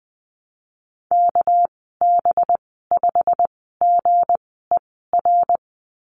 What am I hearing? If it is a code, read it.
Morse code, heard as a keyed tone, about 20 wpm, spaced as for 15 wpm, 700 Hz. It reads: KB5GER
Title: 200 Lesson 32g - Letter k - Call Signs 20wpm